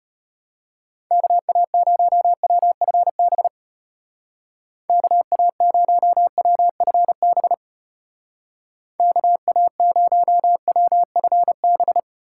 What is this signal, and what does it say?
Morse code, heard as a keyed tone, about 34 wpm, 700 Hz. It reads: KA0WFB KA0WFB KA0WFB